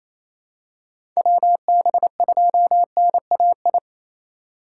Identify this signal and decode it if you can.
Morse code, heard as a keyed tone, about 28 wpm, 700 Hz. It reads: WB2NAI
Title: USA Call Signs - 6 Characters - 4x Character Spacing - 28wpm